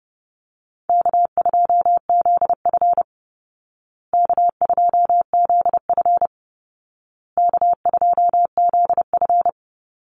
Morse code, heard as a keyed tone, about 30 wpm, 700 Hz. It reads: K2ZF K2ZF K2ZF
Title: Easy Call Signs Repeated 3x - 30wpm